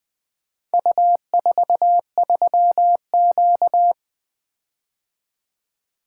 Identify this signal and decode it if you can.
Morse code, heard as a keyed tone, about 20 wpm, 700 Hz. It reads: U43Q
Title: Four Letter-Number Combinations - 20wpm